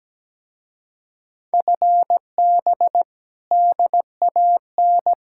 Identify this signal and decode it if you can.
Morse code, heard as a keyed tone, about 17 wpm, 700 Hz. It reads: FB DAN
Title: Speed-Racing Common QSO Phrases - 17wpm